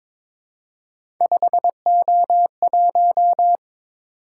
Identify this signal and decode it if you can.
Morse code, heard as a keyed tone, about 22 wpm, 700 Hz. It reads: 5O1
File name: practice / 22wpm / Three Letter-Number Combinations - 22wpm